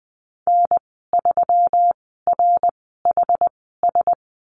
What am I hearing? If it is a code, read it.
Morse code, heard as a keyed tone, about 20 wpm, spaced as for 14 wpm, 700 Hz. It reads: N3RHS
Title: 128 Lesson 21g - Number 6 - Call Signs 20wpm